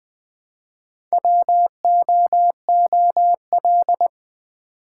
Morse code, heard as a keyed tone, about 20 wpm, 700 Hz. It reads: WOOL
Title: Build A Word - 1 Letter At A Time - 4-Letter Words - 20wpm